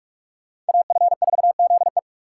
Morse code, heard as a keyed tone, about 57 wpm, 700 Hz. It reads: AF4ZE